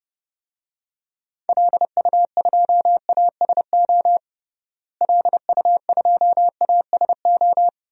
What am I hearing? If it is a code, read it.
Morse code, heard as a keyed tone, about 30 wpm, 700 Hz. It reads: LU2ASO LU2ASO
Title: DXCC-100 Call Signs - 6 Characters - Repeated 3x - 30wpm